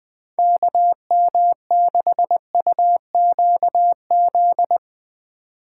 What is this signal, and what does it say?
Morse code, heard as a keyed tone, about 20 wpm, 700 Hz. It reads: KM6UQZ
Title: Speed Racing - USA 2x3 Call Signs 17wpm